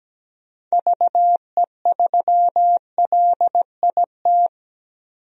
Morse code, heard as a keyed tone, about 17 wpm, 700 Hz. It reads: VE3LIT
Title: DXCC-100 Call Signs - 6 Characters - Repeated 3x - 17wpm